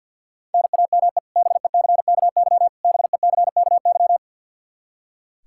Morse code, heard as a keyed tone, about 46 wpm, 700 Hz. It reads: NAME BECKY BECKY
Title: K1USN SST QSO Practice 50wpm